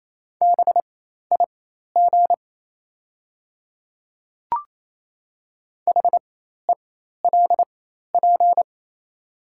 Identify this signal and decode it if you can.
Morse code, heard as a keyed tone, about 28 wpm, spaced as for 13 wpm, 700 Hz. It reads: BIG HELP
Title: 4x Character Spacing - Top 300 Words - 28wpm